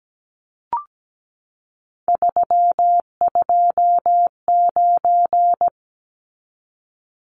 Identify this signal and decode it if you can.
Morse code, heard as a keyed tone, about 17 wpm, 700 Hz. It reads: 329
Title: Numbers - 3-Digit 17wpm